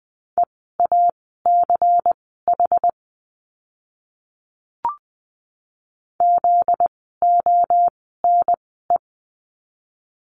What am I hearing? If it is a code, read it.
Morse code, heard as a keyed tone, about 20 wpm, spaced as for 14 wpm, 700 Hz. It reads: EACH ZONE
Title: 244 Lesson 39f - Letter z - Words 20wpm